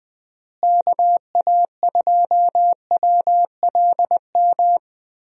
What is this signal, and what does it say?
Morse code, heard as a keyed tone, about 20 wpm, 700 Hz. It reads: KA2WLM
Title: USA Call Signs - 6 Characters - Repeated 3x - 20wpm